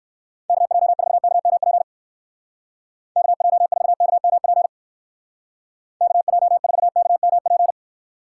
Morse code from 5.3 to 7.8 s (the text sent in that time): KJ4KMP